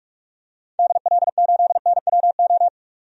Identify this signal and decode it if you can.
Morse code, heard as a keyed tone, about 46 wpm, 700 Hz. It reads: DL8NWO
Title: DXCC-100 Call Signs - 6 Characters - 45wpm